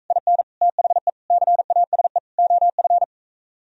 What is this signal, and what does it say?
Morse code, heard as a keyed tone, about 45 wpm, 700 Hz. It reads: IN THE CASE OF